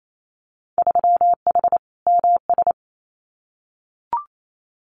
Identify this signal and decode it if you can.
Morse code, heard as a keyed tone, about 28 wpm, 700 Hz. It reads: 3H MS
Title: Winter Field Day Exchanges Encore 28wpm